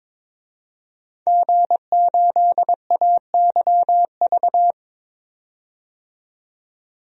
Morse code, heard as a keyed tone, about 22 wpm, 700 Hz. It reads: G8AYV
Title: DXCC-100 Call Signs - 5 Characters - 22wpm